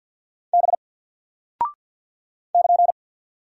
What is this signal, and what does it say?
Morse code, heard as a keyed tone, about 48 wpm, 700 Hz. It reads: B <KN>
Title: Single Character (Letter:Number:Punctuation:Prosign) Alt 50wpm